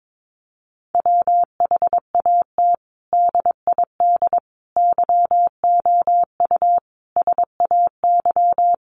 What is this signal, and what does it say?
Morse code, heard as a keyed tone, about 22 wpm, 700 Hz. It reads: WHAT DID YOU SAY
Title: Sentences from Top 200 Words 22wpm